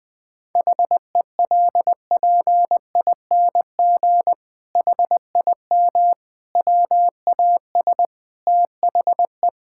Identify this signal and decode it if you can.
Morse code, heard as a keyed tone, about 20 wpm, 700 Hz. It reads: HELPING HIM WAS THE
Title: Sentences with 7-Word Phrases - Spoken Beginning - 20wpm